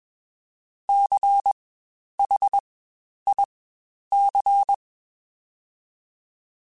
Morse code, heard as a keyed tone, about 21 wpm, spaced as for 10 wpm, 800 Hz. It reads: CHIC
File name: Spaced Alphas / Spaced Alphas mp3s / Cs sp @20